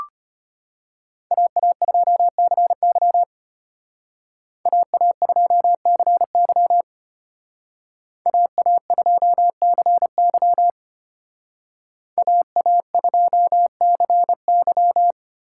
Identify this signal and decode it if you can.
Morse code, heard as a keyed tone, about 31 wpm, 700 Hz. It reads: AA2CY AA2CY AA2CY AA2CY